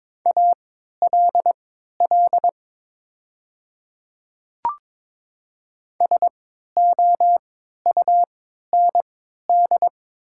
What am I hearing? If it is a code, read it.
Morse code, heard as a keyed tone, about 22 wpm, spaced as for 13 wpm, 700 Hz. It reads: ALL SOUND
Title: 3x Character Spacing - Top 100 Words - 22wpm